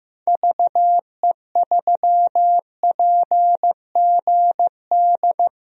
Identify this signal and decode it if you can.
Morse code, heard as a keyed tone, about 15 wpm, 700 Hz. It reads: VE3PGD
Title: DXCC-100 Call Signs - 6 Characters - 15wpm